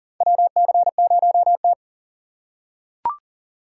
Morse code, heard as a keyed tone, about 40 wpm, 700 Hz. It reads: WC0T
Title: USA Call Signs - 4 Characters - 6x Character Spacing - 40wpm